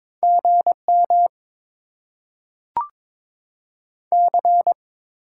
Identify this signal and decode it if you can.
Morse code, heard as a keyed tone, about 22 wpm, 700 Hz. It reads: GM C
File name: Top 20 QSO Elements - 8x Character Spacing - 22wpm